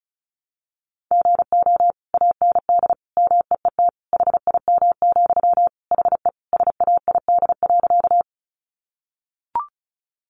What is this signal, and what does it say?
Morse code, heard as a keyed tone, about 34 wpm, 700 Hz. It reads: GO AND MEET HIM, HE SAID.
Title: Sentences from Top 1100 Words 35wpm